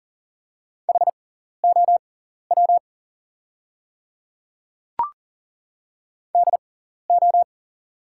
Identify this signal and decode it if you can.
Morse code, heard as a keyed tone, about 39 wpm, spaced as for 14 wpm, 690 Hz. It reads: HOW DO
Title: 6x Character Spacing - Top 100 Words - 40wpm